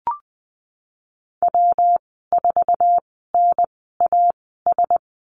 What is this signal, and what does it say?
Morse code, heard as a keyed tone, about 19 wpm, spaced as for 14 wpm, 700 Hz. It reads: W4NAS
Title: 128 Lesson 21g - Number 6 - Call Signs 20wpm